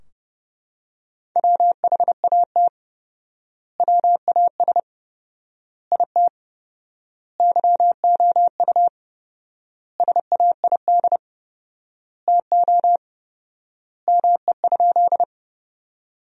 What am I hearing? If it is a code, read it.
Morse code, heard as a keyed tone, about 30 wpm, 700 Hz. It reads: WHAT WAS IT YOU SAID TO ME?
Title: 4x Word-Spacing - Sentences from Top 200 Words - 30wpm